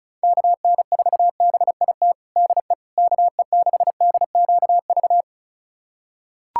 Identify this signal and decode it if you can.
Morse code, heard as a keyed tone, about 34 wpm, 700 Hz. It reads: KN4BIT DE KE6DQV
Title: USA Callsign DE Callsign - 6 Characters - 35wpm